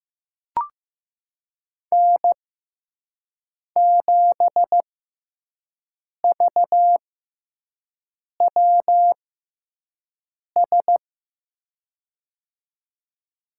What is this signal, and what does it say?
Morse code, heard as a keyed tone, about 15 wpm, spaced as for 5 wpm, 700 Hz. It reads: N7VWS
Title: USA Call Signs - 5 Characters - 6x Character Spacing - 15wpm